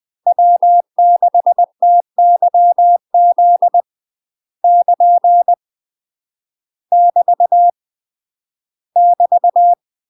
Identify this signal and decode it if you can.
Morse code, heard as a keyed tone, about 20 wpm, 700 Hz. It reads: W6TYZ <KN> <BT> <BT>
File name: UK MASTER QSOs TRAINING / 20WPM / UK MASTER QSOs 0043